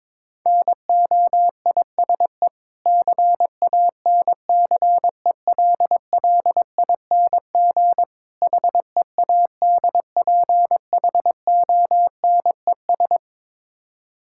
Morse code, heard as a keyed tone, about 22 wpm, 690 Hz. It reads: NOISE CANCELLING HEADPHONES
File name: Difficult Words with Hints - 22wpm